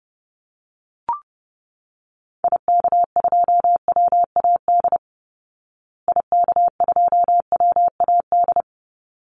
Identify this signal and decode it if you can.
Morse code, heard as a keyed tone, about 28 wpm, 700 Hz. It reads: IK2WAD IK2WAD
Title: Call Signs Repeated 3x - 30wpm